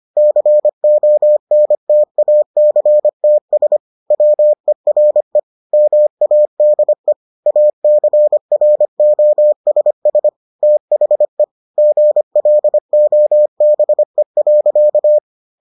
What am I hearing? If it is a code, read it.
Morse code, heard as a keyed tone, about 25 wpm, 600 Hz. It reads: CONTACTS WERE MADE ACROSS THE GLOBE.